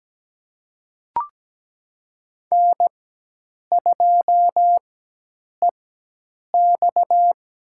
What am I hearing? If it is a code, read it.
Morse code, heard as a keyed tone, about 17 wpm, spaced as for 8 wpm, 700 Hz. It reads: N2EX
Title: USA Call Signs - 4 Characters - 4x Character Spacing - 17wpm